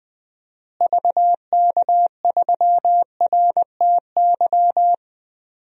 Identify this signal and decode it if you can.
Morse code, heard as a keyed tone, about 20 wpm, 700 Hz. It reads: VK3RTY
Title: DXCC-100 Call Signs - 6 Characters - Repeated 3x - 20wpm